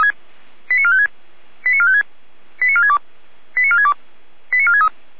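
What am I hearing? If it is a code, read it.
Five-tone selective call (ZVEI), heard as quick runs of five tones, tones of about 70 ms. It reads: 89456 89456 89462 89462 89463
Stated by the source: Five-tone ZVEI selective call received on an SDR.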